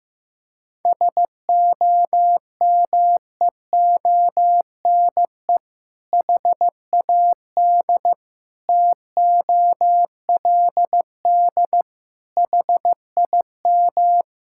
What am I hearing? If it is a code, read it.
Morse code, heard as a keyed tone, about 15 wpm, 700 Hz. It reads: SOMEONE HAD TOLD HIM
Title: Sentences from Top 1000 Words - Review Part 6 - 15wpm